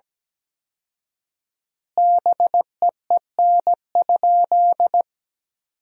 Morse code, heard as a keyed tone, about 17 wpm, 700 Hz. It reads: BEEN?
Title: 4x Word-Spacing - Sentences from Top 400 Words - 17wpm